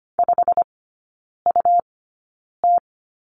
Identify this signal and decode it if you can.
Morse code, heard as a keyed tone, about 25 wpm, spaced as for 9 wpm, 700 Hz. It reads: 5UT
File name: Lesson 17e - Letter u - Three Characters 25wpm